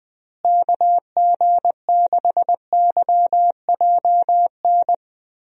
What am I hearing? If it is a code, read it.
Morse code, heard as a keyed tone, about 20 wpm, 700 Hz. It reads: KG6YJN
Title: USA Call Signs - 6 Characters - Repeated 2x - 20wpm